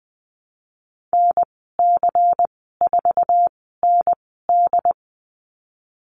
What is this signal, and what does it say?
Morse code, heard as a keyed tone, about 20 wpm, spaced as for 15 wpm, 700 Hz. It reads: NC4ND
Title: 167 Lesson 27g - Number 7 - Call Signs 20wpm